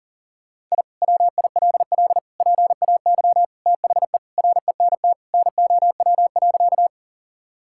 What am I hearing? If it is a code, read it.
Morse code, heard as a keyed tone, about 40 wpm, 700 Hz. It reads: I WILL PAY THE RENT NOW.
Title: Sentences from Top 1000 Words - Review Part 5 - 40wpm